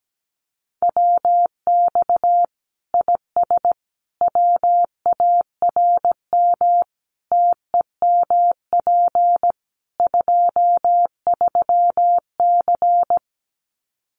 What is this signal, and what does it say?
Morse code, heard as a keyed tone, about 17 wpm, 700 Hz. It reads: WX IS WARM TEMP 23C
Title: Common QSO Phrases 17wpm